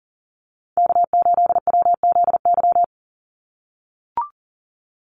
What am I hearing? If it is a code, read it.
Morse code, heard as a keyed tone, about 40 wpm, 700 Hz. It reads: K8WZY
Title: USA Call Signs 40wpm